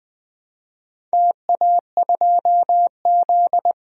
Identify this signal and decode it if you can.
Morse code, heard as a keyed tone, about 21 wpm, 700 Hz. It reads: TA2Z
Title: DXCC-100 Call Signs - 4 Characters - Repeated 3x - 20wpm